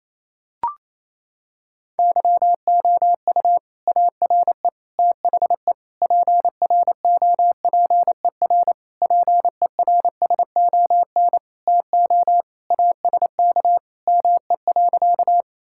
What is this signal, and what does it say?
Morse code, heard as a keyed tone, about 28 wpm, 700 Hz. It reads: YOU ARE THE PROPER PERSON TO ASK ME.
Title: Sentences from Top 1000 Words - Review Part 5 - 28wpm